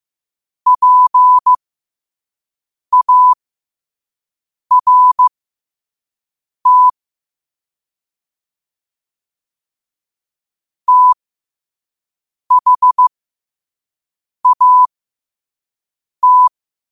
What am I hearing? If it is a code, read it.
Morse code, heard as a keyed tone, about 15 wpm, spaced as for 5 wpm, 1000 Hz. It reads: PART THAT